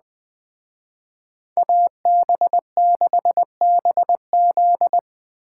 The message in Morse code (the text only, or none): AB6BZ